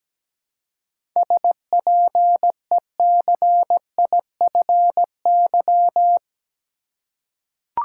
Morse code, heard as a keyed tone, about 17 wpm, 700 Hz. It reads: SPECIFY
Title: Seven Letter Words - 17wpm